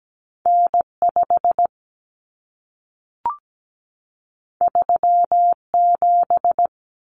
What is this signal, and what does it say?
Morse code, heard as a keyed tone, about 17 wpm, 700 Hz. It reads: N5 37
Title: Common Cut Numbers - 2-Digit 17wpm